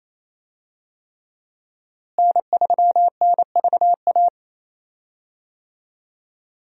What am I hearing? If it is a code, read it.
Morse code, heard as a keyed tone, about 28 wpm, 700 Hz. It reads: N3NVA
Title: USA Call Signs - 5 Characters - Repeated 3x - 28wpm